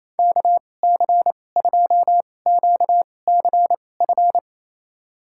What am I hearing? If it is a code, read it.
Morse code, heard as a keyed tone, about 28 wpm, spaced as for 20 wpm, 700 Hz. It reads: KC2QCF
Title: USA Call Signs - 2x Character Spacing - 28wpm